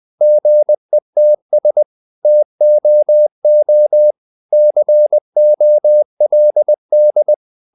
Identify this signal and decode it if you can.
Morse code, heard as a keyed tone, about 20 wpm, 600 Hz. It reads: GETS TOO COLD
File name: the old man and the sea part 2 20wpm